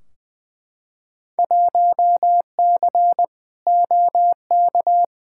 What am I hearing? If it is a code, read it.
Morse code, heard as a keyed tone, about 20 wpm, 700 Hz. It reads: 1C OK
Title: Field Day Exchanges Encore - 20wpm